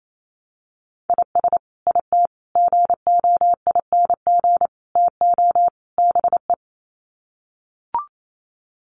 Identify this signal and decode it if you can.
Morse code, heard as a keyed tone, about 28 wpm, 700 Hz. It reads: IS IT GOING TO BE
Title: Sets of 5 Words Encore2 28wpm